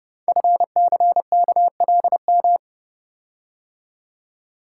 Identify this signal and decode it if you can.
Morse code, heard as a keyed tone, about 29 wpm, 700 Hz. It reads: FCKLM
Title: Code Groups - 5 Letters - 30wpm